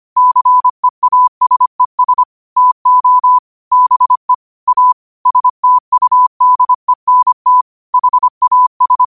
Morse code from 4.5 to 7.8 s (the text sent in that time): A STUDENT